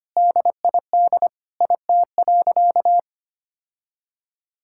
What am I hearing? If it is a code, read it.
Morse code, heard as a keyed tone, about 25 wpm, 700 Hz. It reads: DID IT.
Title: Sentences from Top 1000 Words - Review Part 6 - 25wpm